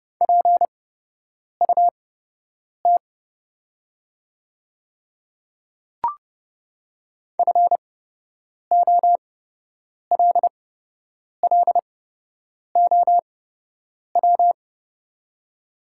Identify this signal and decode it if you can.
Morse code, heard as a keyed tone, about 30 wpm, spaced as for 8 wpm, 700 Hz. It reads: PUT FOLLOW
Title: 8x Character Spacing - Top 300 Words - 30wpm